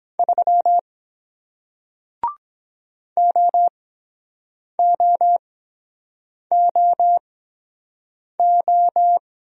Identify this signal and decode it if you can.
Morse code, heard as a keyed tone, about 23 wpm, spaced as for 7 wpm, 700 Hz. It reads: 3 OOOO